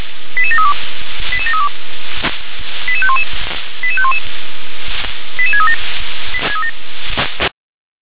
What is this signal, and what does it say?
Five-tone selective call (ZVEI), heard as quick runs of five tones, tones of about 70 ms. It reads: 88732 88732 88511 88511 88637